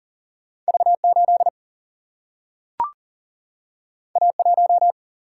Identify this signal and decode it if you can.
Morse code, heard as a keyed tone, about 40 wpm, 700 Hz. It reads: V 8 A 1
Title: Single-Letter Single-Number 40wpm